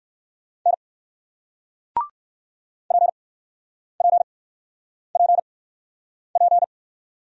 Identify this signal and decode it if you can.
Morse code, heard as a keyed tone, about 59 wpm, spaced as for 9 wpm, 700 Hz. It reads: N PPPP